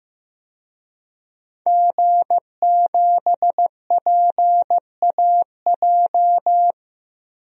Morse code, heard as a keyed tone, about 15 wpm, 700 Hz. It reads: G7PAJ
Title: DXCC-100 Call Signs - 5 Characters - 15wpm